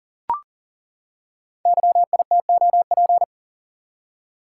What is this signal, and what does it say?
Morse code, heard as a keyed tone, about 38 wpm, 690 Hz. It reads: YITOP
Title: Code Groups - 5 Letters - 40wpm